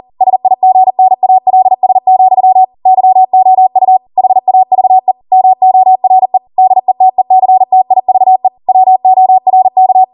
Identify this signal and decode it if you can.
Morse code, heard as a keyed tone, about 40 wpm, 750 Hz. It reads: SIGNALS, YOU HAVE MORE DETECTIVE WORK
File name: Morse / 40WPM